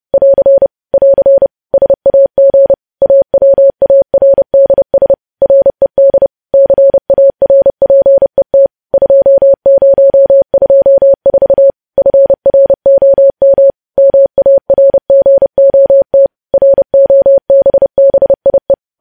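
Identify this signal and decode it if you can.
Morse code, heard as a keyed tone, about 30 wpm, 550 Hz. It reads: <AR> <AR> SAG AWARDS RED CARPET 2024 FROM MARGOT ROBBIE
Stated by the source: Morse news headlines sent at 30 wpm, opening with vvv.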